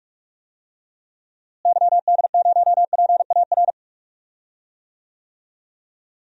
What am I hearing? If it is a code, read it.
Morse code, heard as a keyed tone, about 41 wpm, 700 Hz. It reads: YD0PAR
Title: DXCC-100 Call Signs - 6 Characters - Repeated 2x - 45wpm